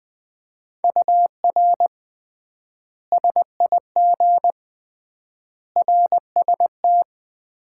Morse code, heard as a keyed tone, about 21 wpm, 700 Hz. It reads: UR SIG RST
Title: Common QSO Phrases - 3x Word-Spacing - 20wpm